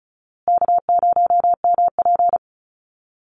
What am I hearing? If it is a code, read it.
Morse code, heard as a keyed tone, about 34 wpm, 700 Hz. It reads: K0MP
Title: Easy Call Signs Repeated 3x - 35wpm